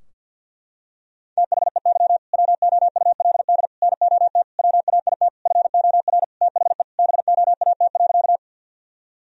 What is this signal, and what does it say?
Morse code, heard as a keyed tone, about 51 wpm, 700 Hz. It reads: THEY WOULD NOT WAIT FOR THE BOAT.
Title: Sentences from Top 1000 Words - Review Part 3 - 50wpm